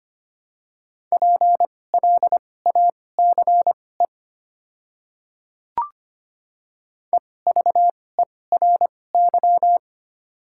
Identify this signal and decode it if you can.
Morse code, heard as a keyed tone, about 25 wpm, spaced as for 18 wpm, 700 Hz. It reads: PLACE EVERY
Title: 2x Character Spacing - Top 300 Words - 25wpm